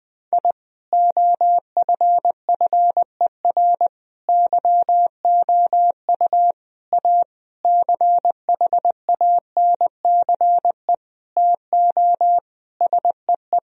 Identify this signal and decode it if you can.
Morse code, heard as a keyed tone, about 20 wpm, 700 Hz. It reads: I OFFER YOU A CHANCE TO SEE
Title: Sentences from Top 1000 Words - Review Part 4 - 20wpm